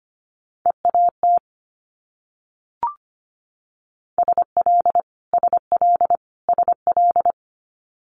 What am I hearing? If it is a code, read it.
Morse code, heard as a keyed tone, about 25 wpm, 700 Hz. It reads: EAT SL SL SL
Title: Multi-letter Phonemes Repeated 3x 25wpm